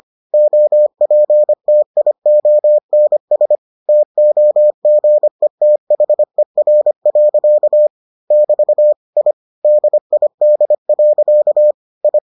Morse code, heard as a keyed tone, about 25 wpm, 600 Hz. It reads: OPTIONS TOGETHER. <BT> I DID. I